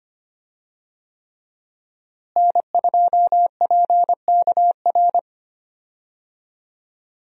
Morse code, heard as a keyed tone, about 25 wpm, 700 Hz. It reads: N2PKR
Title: USA Call Signs - 5 Characters - Repeated 3x - 25wpm